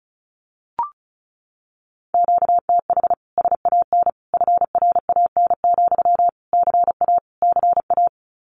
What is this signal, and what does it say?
Morse code, heard as a keyed tone, about 34 wpm, 700 Hz. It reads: QTH SAN FRAN, CA CA
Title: Common QSO Phrases Encore 35wpm